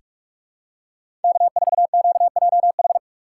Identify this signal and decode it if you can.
Morse code, heard as a keyed tone, about 46 wpm, 700 Hz. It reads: K4QJH
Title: USA Call Signs - 2x Character Spacing - 45wpm